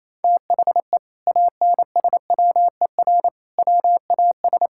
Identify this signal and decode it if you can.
Morse code, heard as a keyed tone, about 28 wpm, 700 Hz. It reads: THE ANSWER WAS
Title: Sentences from Top 1000 Words - Review Part 6 - 28wpm